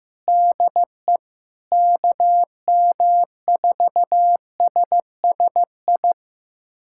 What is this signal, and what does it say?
Morse code, heard as a keyed tone, about 15 wpm, 700 Hz. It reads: DE KM4SSI
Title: USA Callsign DE Callsign 15wpm